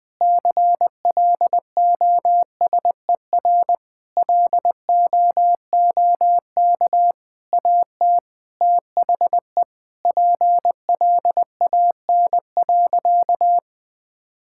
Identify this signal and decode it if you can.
Morse code, heard as a keyed tone, about 19 wpm, 700 Hz. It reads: CLOSER LOOK AT THE PLAN.
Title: Sentences with 6-Word Phrases - Spoken Beginning - 20wpm